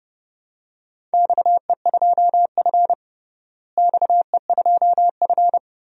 Morse code, heard as a keyed tone, about 32 wpm, 700 Hz. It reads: XE2F XE2F
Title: DXCC-100 Call Signs - 4 Characters - Repeated 3x - 30wpm